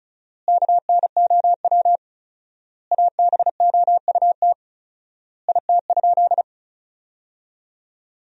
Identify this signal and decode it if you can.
Morse code, heard as a keyed tone, about 36 wpm, 700 Hz. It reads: KNOW ABOUT IT?
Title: 4x Word-Spacing - Sentences from Top 100 Words - 35wpm